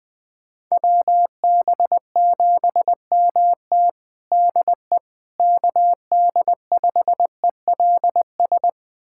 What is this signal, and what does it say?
Morse code, heard as a keyed tone, about 20 wpm, 700 Hz. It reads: WB7MT DE KD5ELS
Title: USA Callsign DE Callsign 20wpm